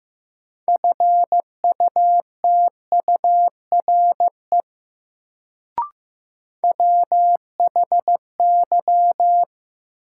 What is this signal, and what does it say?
Morse code, heard as a keyed tone, about 15 wpm, 700 Hz. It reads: FUTURE WHY
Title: Sets of 3 Words - New Word Drill - 15wpm